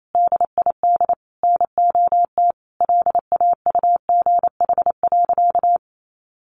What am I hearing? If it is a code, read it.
Morse code, heard as a keyed tone, about 28 wpm, 700 Hz. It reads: DID NOT LAUGH.
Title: Sentences from Top 1100 Words 28wpm